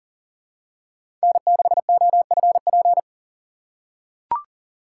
Morse code, heard as a keyed tone, about 40 wpm, 700 Hz. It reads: N6OFP
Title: USA Call Signs - 5 Characters - 3x Character Spacing - 40wpm